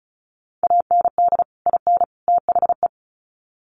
Morse code, heard as a keyed tone, about 34 wpm, 700 Hz. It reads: AND IN THE
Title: Sets of 3 Words 35wpm